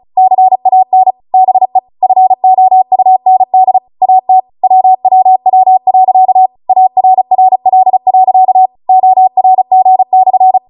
Morse code, heard as a keyed tone, about 36 wpm, 750 Hz. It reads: CAN BE FOUND AT WWW.ARRL.ORG/